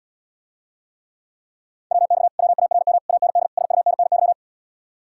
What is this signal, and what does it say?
Morse code, heard as a keyed tone, about 80 wpm, 700 Hz. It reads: WX WARM AND SUNNY